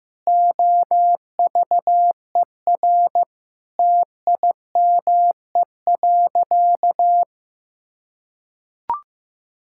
Morse code, heard as a keyed tone, about 15 wpm, 700 Hz. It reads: OVER TIME.
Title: Sentences from Top 1000 Words - Review Part 5 - 15wpm